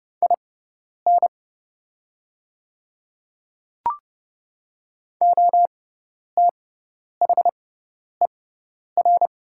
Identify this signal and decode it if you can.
Morse code, heard as a keyed tone, about 30 wpm, spaced as for 10 wpm, 700 Hz. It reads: IN OTHER
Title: 6x Character Spacing - Top 100 Words - 30wpm